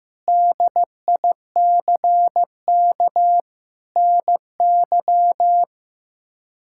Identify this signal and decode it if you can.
Morse code, heard as a keyed tone, about 15 wpm, 700 Hz. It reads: DICK NY
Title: K1USN SST QSO Practice - 15wpm